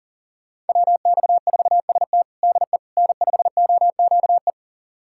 Morse code, heard as a keyed tone, about 40 wpm, 700 Hz. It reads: WX4ST DE N5OQE
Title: USA Callsign DE Callsign 40wpm